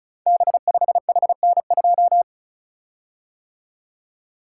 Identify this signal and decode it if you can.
Morse code, heard as a keyed tone, about 36 wpm, 690 Hz. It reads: B5HN2
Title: Code Groups - 5 Letter:Numbers - 35wpm